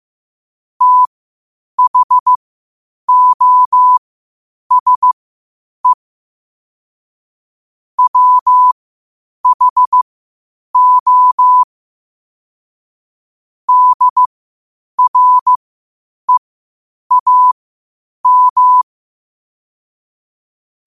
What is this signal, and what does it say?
Morse code, heard as a keyed tone, about 15 wpm, spaced as for 8 wpm, 1000 Hz. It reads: THOSE WHO DREAM
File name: qotdfiles / 8wpm